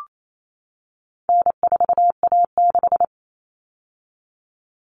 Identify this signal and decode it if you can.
Morse code, heard as a keyed tone, about 28 wpm, 700 Hz. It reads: N4A6
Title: All Cut Numbers - 4-Digit 28wpm